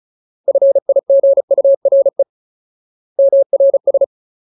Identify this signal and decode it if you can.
Morse code, heard as a keyed tone, about 36 wpm, 550 Hz. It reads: FIGURE MRS
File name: Ultimate Word List / 35WPM Ultimate